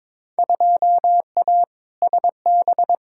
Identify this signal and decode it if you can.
Morse code, heard as a keyed tone, about 22 wpm, 700 Hz. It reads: 2A SB
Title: Field Day Exchanges - 22wpm